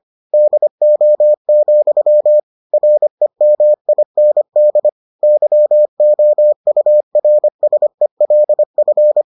Morse code, heard as a keyed tone, about 25 wpm, 600 Hz. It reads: DO, REMIND YOURSELF